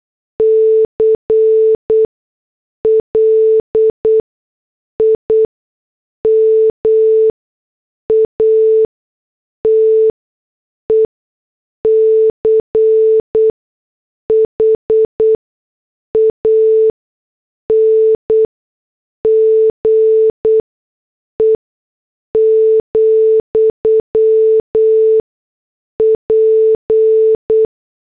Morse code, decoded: CLIMATE CHANGE, P